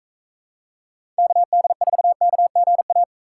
Morse code, heard as a keyed tone, about 40 wpm, 700 Hz. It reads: KD4KGA